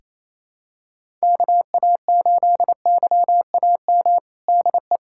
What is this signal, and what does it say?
Morse code, heard as a keyed tone, about 28 wpm, 700 Hz. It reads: KA8YAM DE
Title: USA Callsign DE Callsign - 6 Characters - 28wpm